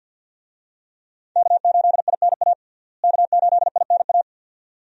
Morse code, heard as a keyed tone, about 52 wpm, 700 Hz. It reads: K8INA K8INA
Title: USA Call Signs - 5 Characters - Repeated 2x - 50wpm